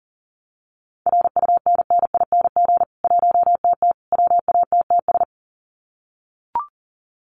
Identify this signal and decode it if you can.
Morse code, heard as a keyed tone, about 40 wpm, 700 Hz. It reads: RUNNING 1TT WATTS
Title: Common QSO Phrases 40wpm